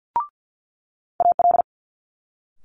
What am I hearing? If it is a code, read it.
Morse code, heard as a keyed tone, about 50 wpm, 700 Hz. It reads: AL
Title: Multi-letter Phonemes Quick Fire 50wpm